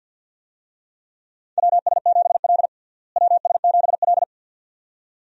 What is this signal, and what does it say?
Morse code, heard as a keyed tone, about 49 wpm, 700 Hz. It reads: WS7L WS7L
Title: USA Call Signs - 4 Characters - Repeated 2x - 50wpm